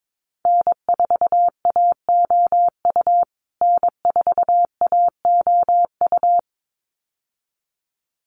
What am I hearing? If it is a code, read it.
Morse code, heard as a keyed tone, about 22 wpm, 700 Hz. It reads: N4AOU N4AOU
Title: USA Call Signs Repeated 2x 22wpm